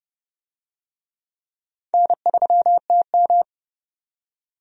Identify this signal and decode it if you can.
Morse code, heard as a keyed tone, about 30 wpm, 700 Hz. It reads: N3TM
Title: USA Call Signs - 4 Characters - 30wpm